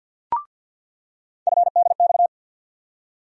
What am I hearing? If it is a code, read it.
Morse code, heard as a keyed tone, about 48 wpm, 690 Hz. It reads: FDX
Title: USA Call Sign Postfixes - 3 Letters 50wpm